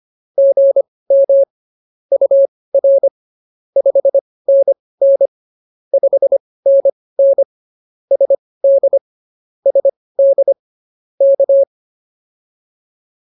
Morse code, heard as a keyed tone, about 24 wpm, spaced as for 18 wpm, 550 Hz. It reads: GM UR 5NN 5NN SD SD K
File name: POTA101 18